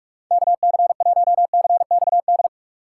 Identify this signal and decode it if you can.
Morse code, heard as a keyed tone, about 44 wpm, 700 Hz. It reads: KC1CXD
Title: USA Call Signs - 6 Characters - 6x Character Spacing - 45wpm